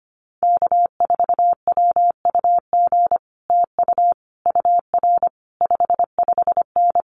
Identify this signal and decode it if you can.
Morse code, heard as a keyed tone, about 25 wpm, 700 Hz. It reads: K4WUG TU UR 55N
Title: Common QSO Phrases Encore 25wpm